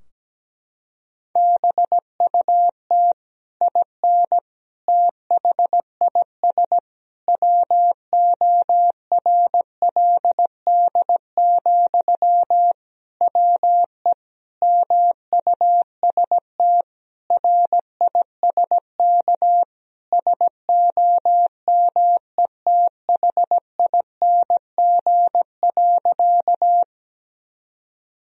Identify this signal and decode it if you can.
Morse code, heard as a keyed tone, about 17 wpm, 700 Hz. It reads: BUT IN THIS WORLD, WE MUST RISK SOMETHING.
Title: Sentences from Top 1000 Words - Review Part 4 - 17wpm